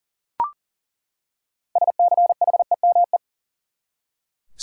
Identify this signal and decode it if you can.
Morse code, heard as a keyed tone, about 40 wpm, 700 Hz. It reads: SCHEME
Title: Six Letter Words - 40wpm